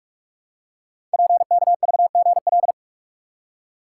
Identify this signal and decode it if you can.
Morse code, heard as a keyed tone, about 44 wpm, 700 Hz. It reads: PKVGL
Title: Code Groups - 5 Letters - 45wpm